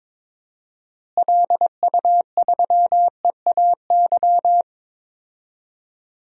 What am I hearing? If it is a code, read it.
Morse code, heard as a keyed tone, about 22 wpm, 700 Hz. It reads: LU3EAY
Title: DXCC-100 Call Signs - 6 Characters - Repeated 3x - 22wpm